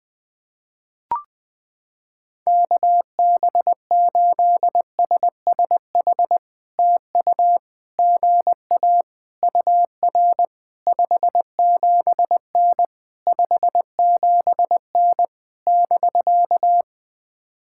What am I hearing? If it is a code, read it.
Morse code, heard as a keyed tone, about 20 wpm, 700 Hz. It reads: KB8SSH TU GA UR 57N 57N <BK>
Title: POTA QSO Practice - 20wpm